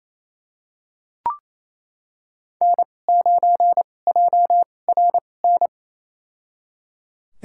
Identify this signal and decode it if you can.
Morse code, heard as a keyed tone, about 28 wpm, spaced as for 20 wpm, 700 Hz. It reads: N9JRN
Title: USA Call Signs - 5 Characters - 2x Character Spacing - 28wpm